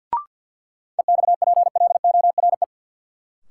Morse code, heard as a keyed tone, about 50 wpm, 700 Hz. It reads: EXPLORE
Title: Sentences from Top 1100 Words - New Word Drill - 50wpm